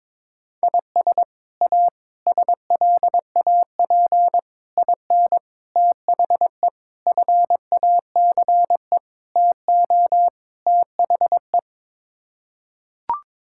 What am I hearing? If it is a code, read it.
Morse code, heard as a keyed tone, about 22 wpm, 700 Hz. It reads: IS A SLAP IN THE FACE TO THE
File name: practice / 22wpm / Sets of 8 Words - 22wpm